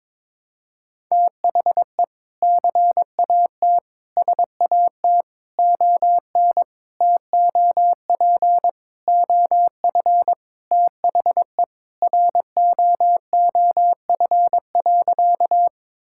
Morse code, heard as a keyed tone, about 21 wpm, 700 Hz. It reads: THE CAT SAT ON TOP OF THE ROOF.
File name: practice / 22wpm / Sentences with 3-Word Phrases - 22wpm